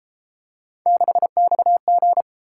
Morse code, heard as a keyed tone, about 31 wpm, 700 Hz. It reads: 6XG